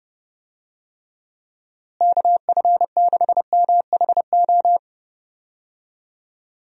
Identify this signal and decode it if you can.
Morse code, heard as a keyed tone, about 30 wpm, 700 Hz. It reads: KF6MHO
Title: USA Call Signs - 6 Characters - Repeated 2x - 30wpm